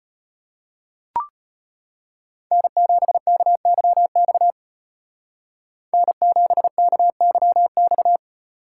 Morse code, heard as a keyed tone, about 32 wpm, 700 Hz. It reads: N7KYX N7KYX